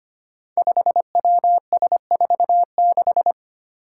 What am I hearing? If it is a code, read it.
Morse code, heard as a keyed tone, about 25 wpm, 700 Hz. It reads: 5WS46
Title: Five Challenging Letter-Number Combinations 25wpm